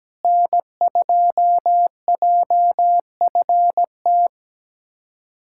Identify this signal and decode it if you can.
Morse code, heard as a keyed tone, about 17 wpm, 700 Hz. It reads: N2JFT
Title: USA Callsign DE Callsign 17wpm